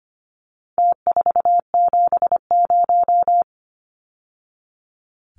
Morse code, heard as a keyed tone, about 25 wpm, 700 Hz. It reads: T470
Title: Numbers - Contest Style - 4-Digit 25wpm